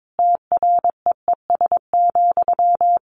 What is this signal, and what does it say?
Morse code, heard as a keyed tone, about 22 wpm, 700 Hz. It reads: TREES,
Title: A Sound Of Thunder 22wpm